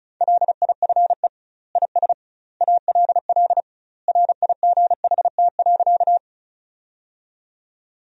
Morse code, heard as a keyed tone, about 36 wpm, 700 Hz. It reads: LIFE IS ALL RIGHT.
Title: 2x Word-Spacing - Sentences from Top 300 Words - 35wpm